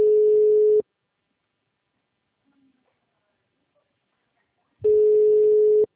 Ringback tone (Europe, 425 Hz).
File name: out- -2011- - - .43503